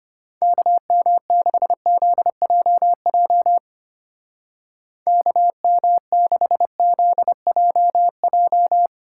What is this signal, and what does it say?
Morse code, heard as a keyed tone, about 31 wpm, 700 Hz. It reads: KM6ZJJ KM6ZJJ
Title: Speed Racing - USA 2x3 Call Signs 25wpm